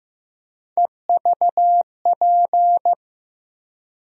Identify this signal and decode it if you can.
Morse code, heard as a keyed tone, about 15 wpm, 700 Hz. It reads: EVP
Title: USA Call Sign Postfixes - 3 Letters 15wpm